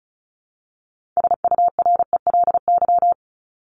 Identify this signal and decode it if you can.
Morse code, heard as a keyed tone, about 34 wpm, 700 Hz. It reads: SURELY
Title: LY Suffix - Top 100 Words with Base Words 35wpm